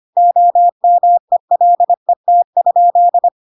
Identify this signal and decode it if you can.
Morse code, heard as a keyed tone, about 25 wpm, 700 Hz. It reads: OMELET?
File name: Pinocchio / Pinocchio ch5 25 25 ik1hge